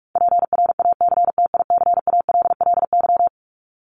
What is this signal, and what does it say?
Morse code, heard as a keyed tone, about 44 wpm, 700 Hz. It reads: PRACTICALLY
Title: LY Suffix - Top 100 Words with Base Words 45wpm